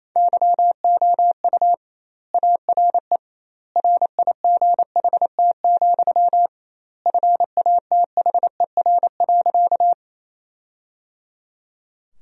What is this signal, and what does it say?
Morse code, heard as a keyed tone, about 27 wpm, 700 Hz. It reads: YOU ARE RIGHT, FATHER.
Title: 2x Word-Spacing - Sentences from Top 300 Words - 28wpm